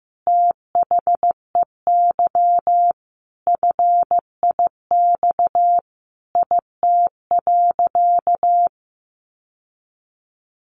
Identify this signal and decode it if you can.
Morse code, heard as a keyed tone, about 15 wpm, 700 Hz. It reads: THEY FIX IT.
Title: Sentences from Top 1400 Words 15wpm